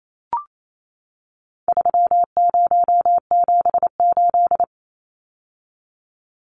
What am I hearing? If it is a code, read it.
Morse code, heard as a keyed tone, about 28 wpm, 700 Hz. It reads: 3078
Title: Numbers - 4-Digit 28wpm